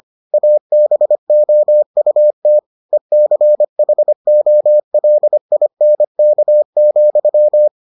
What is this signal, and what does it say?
Morse code, heard as a keyed tone, about 25 wpm, 600 Hz. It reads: ABOUT ECHOLINK,